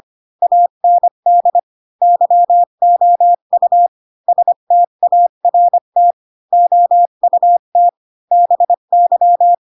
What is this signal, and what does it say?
Morse code, heard as a keyed tone, about 25 wpm, 700 Hz. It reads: AND YOU START OUT BY
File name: Pinocchio / Pinocchio ch3 20 25 ik1hge